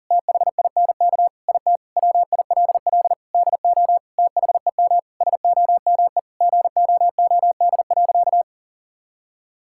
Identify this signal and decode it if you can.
Morse code, heard as a keyed tone, about 40 wpm, 700 Hz. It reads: THINK IT WILL DO THEM SOME GOOD.
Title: Sentences from Top 1000 Words - Review Part 6 - 40wpm